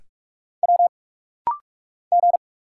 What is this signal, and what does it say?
Morse code, heard as a keyed tone, about 46 wpm, 700 Hz. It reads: W G
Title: Single Letters - Rapid-Fire 45wpm